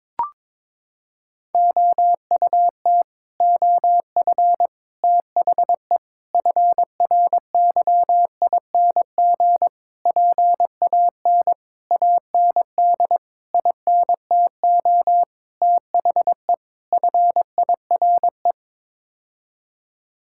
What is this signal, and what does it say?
Morse code, heard as a keyed tone, about 22 wpm, 700 Hz. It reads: OUT OF THE FRYING PAN AND INTO THE FIRE
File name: Sets of 9 Words - 22wpm